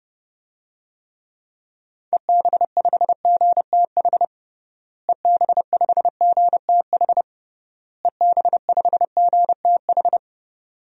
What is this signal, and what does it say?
Morse code, heard as a keyed tone, about 30 wpm, 700 Hz. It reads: EB5GTH EB5GTH EB5GTH
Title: DXCC-100 Call Signs - 6 Characters - Repeated 3x - 30wpm